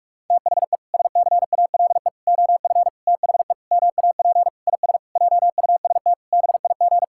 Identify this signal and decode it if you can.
Morse code, heard as a keyed tone, about 41 wpm, 700 Hz. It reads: THE SCALE OF THE MAP IS JUST BIG
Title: Sentences from Top 1000 Words - Review Part 5 - 45wpm